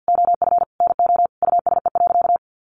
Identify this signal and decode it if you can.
Morse code, heard as a keyed tone, about 50 wpm, 700 Hz. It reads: OF NO USE.
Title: Sentences from Top 100 Words 50wpm